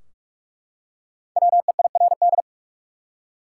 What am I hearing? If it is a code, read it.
Morse code, heard as a keyed tone, about 42 wpm, 700 Hz. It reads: WEIRD
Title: Five Letter Words - 45wpm